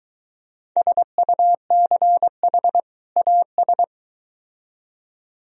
Morse code, heard as a keyed tone, about 22 wpm, 700 Hz. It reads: SUCH AS